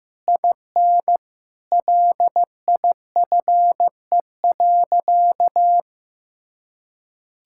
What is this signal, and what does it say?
Morse code, heard as a keyed tone, about 15 wpm, 700 Hz. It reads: IN LIFE.
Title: Sentences from Top 1000 Words - Review Part 6 - 15wpm